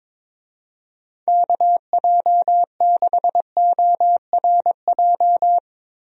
Morse code, heard as a keyed tone, about 22 wpm, 700 Hz. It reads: KJ6ORJ
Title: USA Call Signs - 6 Characters - Repeated 2x - 22wpm